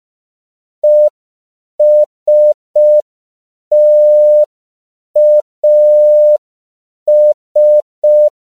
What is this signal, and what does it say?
Morse code, heard as a keyed tone, about 5 wpm, 600 Hz. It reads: ESTAS
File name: CW / 5PPM